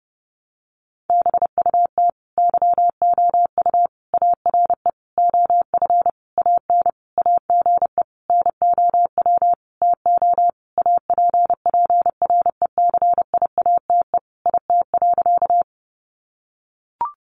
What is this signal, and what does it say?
Morse code, heard as a keyed tone, about 28 wpm, 700 Hz. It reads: BUT YOU ARE OF AN AGE NOW TO APPRECIATE IT.
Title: Sentences from Top 1300 Words 30wpm